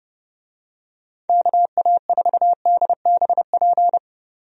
Morse code, heard as a keyed tone, about 30 wpm, 700 Hz. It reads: KA4DBP
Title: USA Call Signs - 6 Characters - 30wpm